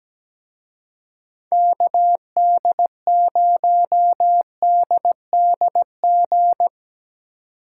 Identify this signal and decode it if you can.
Morse code, heard as a keyed tone, about 18 wpm, 700 Hz. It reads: KD0DDG
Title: Speed Racing - USA 2x3 Call Signs 17wpm